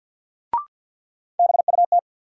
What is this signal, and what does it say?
Morse code, heard as a keyed tone, about 54 wpm, 700 Hz. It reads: BUT
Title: Sentences from Top 100 Words - New Word Drill - 50wpm